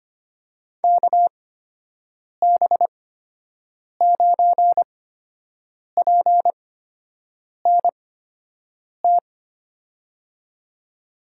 Morse code, heard as a keyed tone, about 25 wpm, spaced as for 7 wpm, 700 Hz. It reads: KB9PNT
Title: USA Call Signs - 6 Characters - 8x Character Spacing - 25wpm